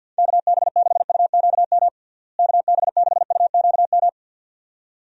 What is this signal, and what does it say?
Morse code, heard as a keyed tone, about 52 wpm, 700 Hz. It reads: KB6UQM KB6UQM
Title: USA Call Signs - 6 Characters - Repeated 2x - 50wpm